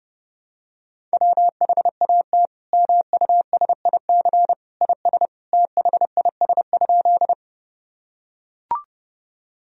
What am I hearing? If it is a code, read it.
Morse code, heard as a keyed tone, about 30 wpm, 700 Hz. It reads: WHAT MUSIC IS THIS?
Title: Sentences from Top 1000 Words - Review Part 1 - 30wpm